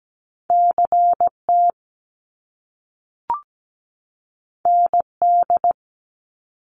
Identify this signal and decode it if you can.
Morse code, heard as a keyed tone, about 17 wpm, 700 Hz. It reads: CT ND
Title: US States 17wpm